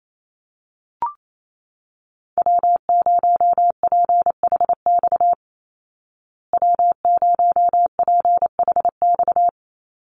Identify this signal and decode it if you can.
Morse code, heard as a keyed tone, about 28 wpm, 700 Hz. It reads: W0PHX W0PHX
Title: Call Signs Repeated 3x - 28wpm